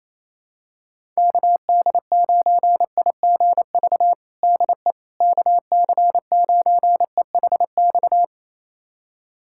Morse code, heard as a keyed tone, about 28 wpm, 700 Hz. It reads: KD9IGV DE KC9EHX
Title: USA Callsign DE Callsign - 6 Characters - 28wpm